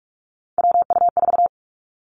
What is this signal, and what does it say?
Morse code, heard as a keyed tone, about 46 wpm, 700 Hz. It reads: WU4
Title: Three Challenging Letter-Number Combinations 45wpm